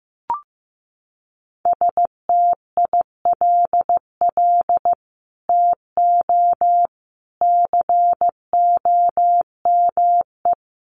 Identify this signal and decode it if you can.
Morse code, heard as a keyed tone, about 15 wpm, 700 Hz. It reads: STILL TO COME
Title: Sets of 5 Words Encore2 15wpm